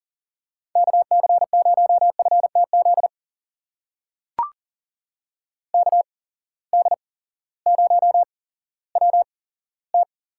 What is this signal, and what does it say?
Morse code, heard as a keyed tone, about 40 wpm, spaced as for 18 wpm, 700 Hz. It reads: KC0FTZ KD0WT
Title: USA Call Signs - 6 Characters - 8x Character Spacing - 40wpm